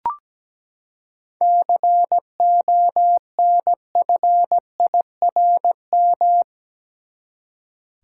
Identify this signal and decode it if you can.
Morse code, heard as a keyed tone, about 17 wpm, 700 Hz. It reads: CONFIRM
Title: Seven Letter Words - 17wpm